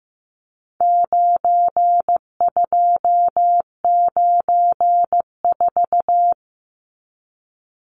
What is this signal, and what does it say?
Morse code, heard as a keyed tone, about 15 wpm, 700 Hz. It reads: 9294
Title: Numbers - 4-Digit 15wpm